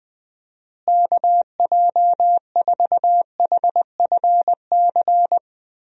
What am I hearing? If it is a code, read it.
Morse code, heard as a keyed tone, about 20 wpm, 700 Hz. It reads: KJ4HFC
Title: USA Call Signs - 6 Characters - Repeated 3x - 20wpm